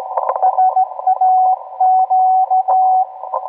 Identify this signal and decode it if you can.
Morse code, heard as a keyed tone, about 16 wpm, 750 Hz. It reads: SAQ